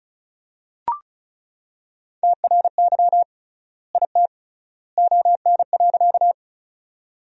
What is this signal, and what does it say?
Morse code, heard as a keyed tone, about 34 wpm, 700 Hz. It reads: TRY IT ON.
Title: 3x Word-Spacing - Sentences from Top 200 Words - 35wpm